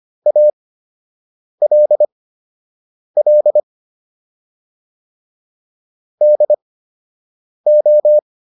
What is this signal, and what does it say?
Morse code, heard as a keyed tone, about 25 wpm, spaced as for 7 wpm, 600 Hz. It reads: ALL DO